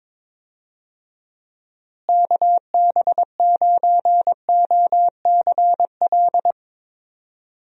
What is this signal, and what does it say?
Morse code, heard as a keyed tone, about 22 wpm, 700 Hz. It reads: KB9OCL
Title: USA Call Signs - 6 Characters - Repeated 3x - 22wpm